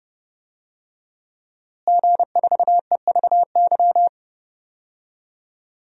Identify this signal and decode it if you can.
Morse code, heard as a keyed tone, about 30 wpm, 700 Hz. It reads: G4EVY
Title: DXCC-100 Call Signs - 5 Characters - Repeated 3x - 30wpm